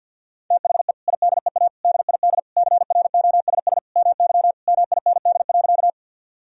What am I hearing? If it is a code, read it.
Morse code, heard as a keyed tone, about 49 wpm, 700 Hz. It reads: THE IDEA DID CROSS MY MIND.
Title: Sentences from Top 1000 Words - Review Part 1 - 50wpm